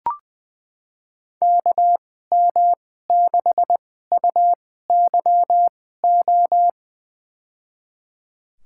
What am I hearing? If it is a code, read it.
Morse code, heard as a keyed tone, about 20 wpm, spaced as for 14 wpm, 700 Hz. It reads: KM6UYO
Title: USA Call Signs - 2x Character Spacing - 20wpm